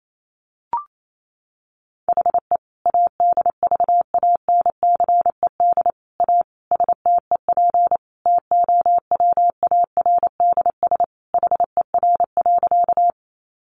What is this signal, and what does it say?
Morse code, heard as a keyed tone, about 28 wpm, 700 Hz. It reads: HE ADVANCED A STEP TOWARDS HER.
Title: Sentences from Top 1200 Words 28wpm